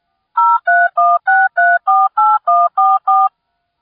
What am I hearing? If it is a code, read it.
DTMF, heard as telephone touch tones, tones of about 220 ms. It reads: 0316348144